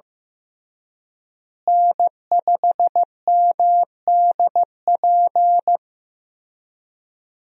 Morse code, heard as a keyed tone, about 15 wpm, 700 Hz. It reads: N5MDP
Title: USA Call Signs - 5 Characters - Repeated 3x - 15wpm